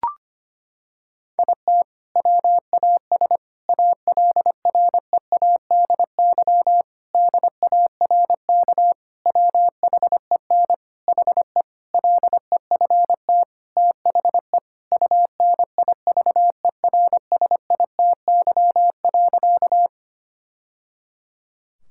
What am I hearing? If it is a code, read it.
Morse code, heard as a keyed tone, about 25 wpm, 700 Hz. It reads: IT WAS ALREADY DARK WHEN HE LEFT THE UNIVERSITY.
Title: Sentences from Top 1000 Words - Review Part 5 - 25wpm